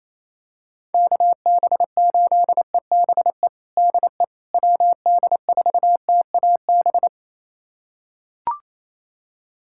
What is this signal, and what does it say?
Morse code, heard as a keyed tone, about 28 wpm, 700 Hz. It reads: KB8EBE DE WD4TAB
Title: USA Callsign DE Callsign - 6 Characters - 28wpm